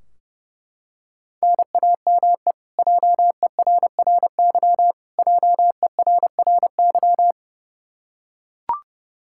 Morse code, heard as a keyed tone, about 30 wpm, 700 Hz. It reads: NAME JERRY JERRY
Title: Practice QSO - Exchange 1 - Beginning - 30wpm